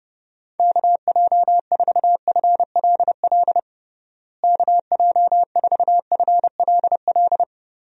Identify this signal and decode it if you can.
Morse code, heard as a keyed tone, about 30 wpm, 700 Hz. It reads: KJ4FLL KJ4FLL
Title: USA Call Signs - 6 Characters - Repeated 3x - 30wpm